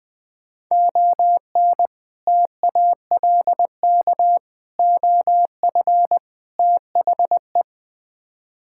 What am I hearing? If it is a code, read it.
Morse code, heard as a keyed tone, about 20 wpm, 690 Hz. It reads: ON TALK OF THE
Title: Build A Phrase - 5-Word Phrases - 20wpm